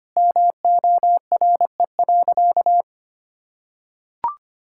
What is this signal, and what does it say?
Morse code, heard as a keyed tone, about 25 wpm, 700 Hz. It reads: MORE.
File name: practice / 25wpm / Sentences with 2-Word Phrases - 25wpm